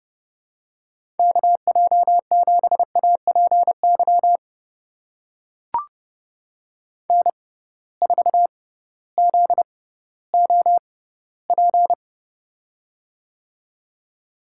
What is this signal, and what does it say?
Morse code, heard as a keyed tone, about 30 wpm, spaced as for 16 wpm, 700 Hz. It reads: KJ7APY N4ZOP
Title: USA Call Signs - 6x Character Spacing - 30wpm